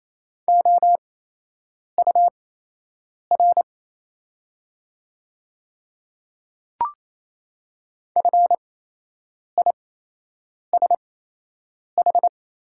Morse code, heard as a keyed tone, about 28 wpm, spaced as for 8 wpm, 690 Hz. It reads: OUR FISH
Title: 8x Character Spacing - Top 500 Words - 28wpm